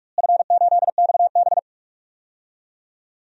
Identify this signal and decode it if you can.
Morse code, heard as a keyed tone, about 45 wpm, 700 Hz. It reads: F8XB
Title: DXCC-100 Call Signs - 4 Characters - Repeated 3x - 45wpm